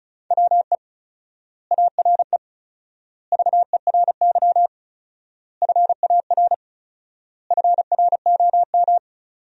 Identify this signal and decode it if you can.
Morse code, heard as a keyed tone, about 34 wpm, 700 Hz. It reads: WE ARE VERY FAR FROM
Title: 4x Word-Spacing - Sentences from Top 300 Words - 35wpm